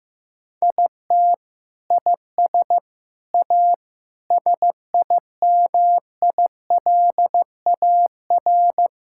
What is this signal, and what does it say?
Morse code, heard as a keyed tone, about 15 wpm, 700 Hz. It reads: IT IS A SIMILAR
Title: Sentences from Top 1000 Words - Review Part 5 - 15wpm